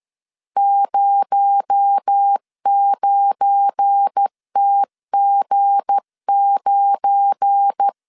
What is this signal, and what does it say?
Morse code, heard as a keyed tone, about 13 wpm, 800 Hz. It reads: 09TG9
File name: LEZIONI / 7 GF9Z / 7 GF9 12wpm 2319